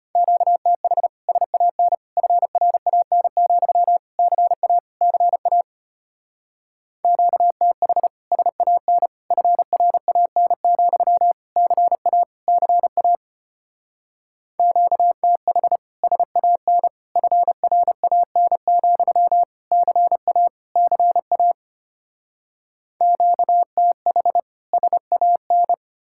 Morse code, heard as a keyed tone, about 33 wpm, 700 Hz. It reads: QTH SAN FRAN, CA CA QTH SAN FRAN, CA CA QTH SAN FRAN, CA CA QTH SAN